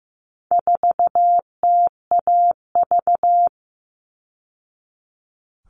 Morse code, heard as a keyed tone, about 15 wpm, 700 Hz. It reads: 4TAV
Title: All Cut Numbers - 4-Digit 15wpm